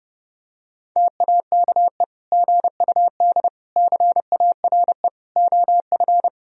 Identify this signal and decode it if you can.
Morse code, heard as a keyed tone, about 30 wpm, 700 Hz. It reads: TAKE GUD CARE OF
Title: Sentences from Top 400 Words and Abbreviations 30wpm